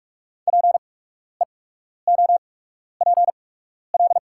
Morse code, heard as a keyed tone, about 44 wpm, spaced as for 12 wpm, 700 Hz. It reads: PEOPL
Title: 8x Character Spacing - Top 100 Words - 45wpm